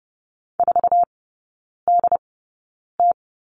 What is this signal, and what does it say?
Morse code, heard as a keyed tone, about 30 wpm, spaced as for 9 wpm, 700 Hz. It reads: 4DT
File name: Lesson 19e - Letter w - Three Characters 30wpm